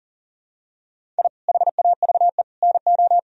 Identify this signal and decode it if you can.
Morse code, heard as a keyed tone, about 40 wpm, 700 Hz. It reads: I HAVE NO
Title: Sentences from Top 1000 Words - Review Part 1 - 40wpm